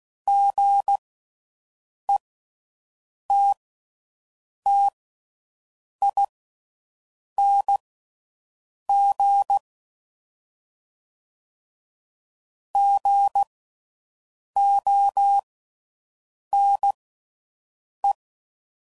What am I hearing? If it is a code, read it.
Morse code, heard as a keyed tone, about 16 wpm, spaced as for 6 wpm, 800 Hz. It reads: GETTING GONE
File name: Mixed / Short Runs / Gotten